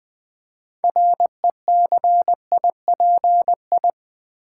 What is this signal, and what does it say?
Morse code, heard as a keyed tone, about 20 wpm, 700 Hz. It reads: RECIPI
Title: Build A Word - 3 Letters At A Time - 9-Letter Words - 20wpm